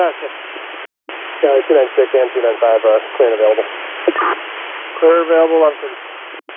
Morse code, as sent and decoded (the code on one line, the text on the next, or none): none